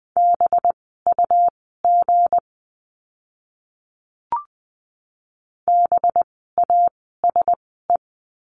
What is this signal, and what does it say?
Morse code, heard as a keyed tone, about 20 wpm, spaced as for 15 wpm, 700 Hz. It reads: BUG BASE
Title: 185 Lesson 30f - Letter b - Words 20wpm